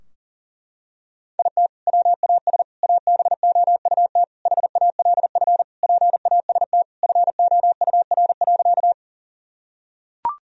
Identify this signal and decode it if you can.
Morse code, heard as a keyed tone, about 37 wpm, 700 Hz. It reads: IT WAS ABOUT HALF PAST FOUR.
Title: Sentences from Top 1000 Words - Review Part 3 - 40wpm